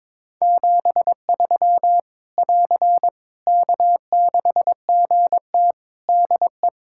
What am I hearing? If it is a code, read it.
Morse code, heard as a keyed tone, about 22 wpm, 700 Hz. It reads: 73 <AR> K6GT DE
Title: Practice QSO - Exchange 4 - Ending - 22wpm